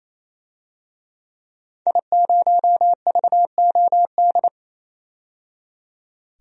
Morse code, heard as a keyed tone, about 28 wpm, 700 Hz. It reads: I0VOD
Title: DXCC-100 Call Signs - 5 Characters - 28wpm